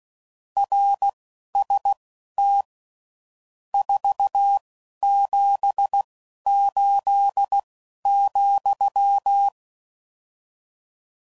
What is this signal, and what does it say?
Morse code, heard as a keyed tone, about 16 wpm, spaced as for 11 wpm, 800 Hz. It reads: RST 478,
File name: CW Code Groups @10WPM / RSTs / Your RST's